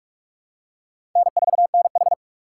Morse code, heard as a keyed tone, about 46 wpm, 700 Hz. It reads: N4NH
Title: USA Call Signs - 4x Character Spacing - 45wpm